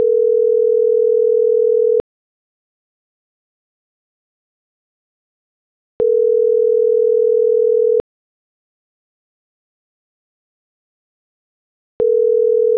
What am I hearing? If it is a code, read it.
Ringback tone (North America), 440 + 480 Hz.